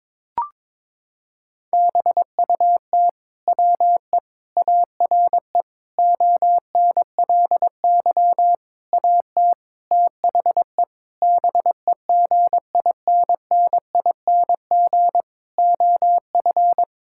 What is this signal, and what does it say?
Morse code, heard as a keyed tone, about 22 wpm, 700 Hz. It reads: BUT WE ARE ONLY AT THE BEGINNING OF
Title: Sentences from Top 1000 Words - Review Part 5 - 22wpm